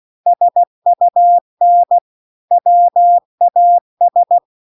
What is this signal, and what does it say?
Morse code, heard as a keyed tone, about 16 wpm, 700 Hz. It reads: SUN WAS